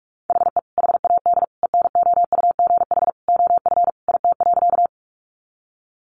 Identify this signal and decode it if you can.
Morse code, heard as a keyed tone, about 44 wpm, 700 Hz. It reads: HE HAD ENOUGH OF IT.
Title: Sentences from Top 400 Words 45wpm